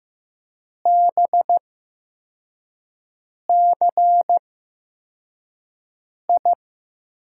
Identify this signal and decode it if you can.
Morse code, heard as a keyed tone, about 15 wpm, spaced as for 4 wpm, 700 Hz. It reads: BCI
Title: USA Call Signs - 6 Characters - 8x Character Spacing - 15wpm